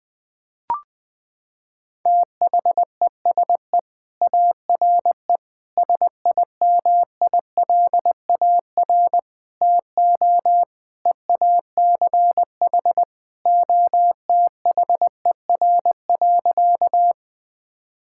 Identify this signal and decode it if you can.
Morse code, heard as a keyed tone, about 21 wpm, 700 Hz. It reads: THESE ARE SIMILAR TO EACH OTHER.
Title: Sentences from Top 1000 Words - Review Part 5 - 20wpm